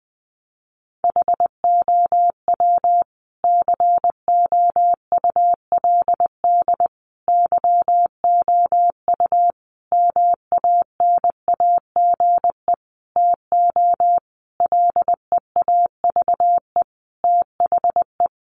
Morse code, heard as a keyed tone, about 20 wpm, 700 Hz. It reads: HOW COULD YOU MANAGE TO LEAVE THE
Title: Sentences from Top 1400 Words 20wpm